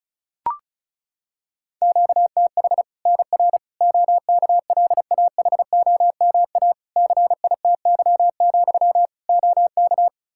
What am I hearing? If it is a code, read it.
Morse code, heard as a keyed tone, about 36 wpm, 700 Hz. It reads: QTH NR OKLAHOMA CITY, OK
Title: Practice QSO - Exchange 1 - Beginning - 35wpm